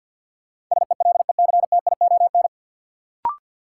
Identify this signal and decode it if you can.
Morse code, heard as a keyed tone, about 50 wpm, 700 Hz. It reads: SELECTION
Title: Sentences from Top 1400 Words - New Word Drill - 50wpm